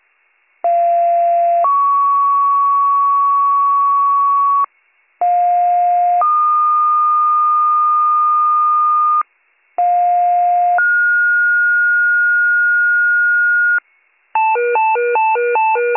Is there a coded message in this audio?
Yes: two-tone page from 0.6 to 13.8 s, heard as a one-second tone, then a three-second tone, about 690 then 1120 Hz, about 710 then 1220 Hz, about 710 then 1430 Hz.